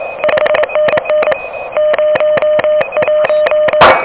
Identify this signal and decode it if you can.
Morse code, heard as a keyed tone, about 23 wpm, 600 Hz, with noise. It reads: 5NN 01
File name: OK QRP 2023 / OK QRP 2023